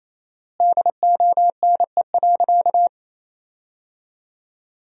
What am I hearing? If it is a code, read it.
Morse code, heard as a keyed tone, about 28 wpm, 700 Hz. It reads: DONE.
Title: 4x Word-Spacing - Sentences from Top 500 Words - 28wpm